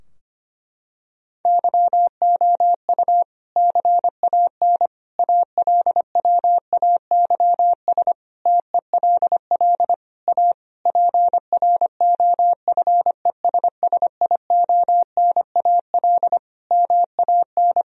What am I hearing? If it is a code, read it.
Morse code, heard as a keyed tone, about 25 wpm, 700 Hz. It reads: YOU CAN ALWAYS TELL A PROFESSIONAL MAN
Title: Sentences from Top 1000 Words - Review Part 4 - 25wpm